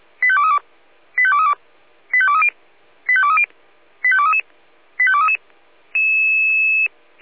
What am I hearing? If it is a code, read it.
Five-tone selective call (ZVEI), heard as quick runs of five tones, tones of about 70 ms. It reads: 85232 85232 85230 85230 85233 85233